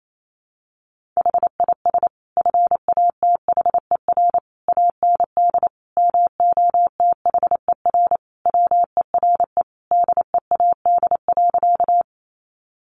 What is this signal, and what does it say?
Morse code, heard as a keyed tone, about 28 wpm, 700 Hz. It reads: HIS FATHER AND MOTHER WERE DEAD.
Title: Sentences from Top 1500 Words 28wpm